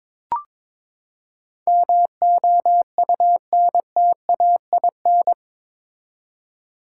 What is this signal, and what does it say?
Morse code, heard as a keyed tone, about 22 wpm, 700 Hz. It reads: MOUNTAIN
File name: practice / 22wpm / Eight Letter Words - 22wpm